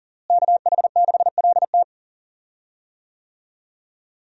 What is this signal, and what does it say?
Morse code, heard as a keyed tone, about 37 wpm, 700 Hz. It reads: KH6LT
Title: USA Call Signs - 5 Characters - Repeated 2x - 40wpm